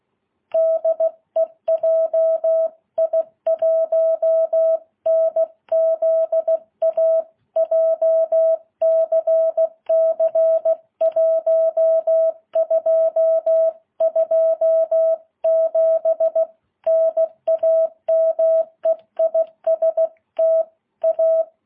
Morse code, heard as a keyed tone, about 16 wpm, 660 Hz. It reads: DE JI1NZA JCC 1227 NAME IS TA
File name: jh8cbh